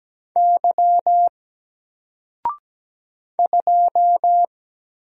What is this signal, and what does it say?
Morse code, heard as a keyed tone, about 17 wpm, 700 Hz. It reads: Y 2
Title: Single Character (Letter:Number:Punctuation:Prosign) 17wpm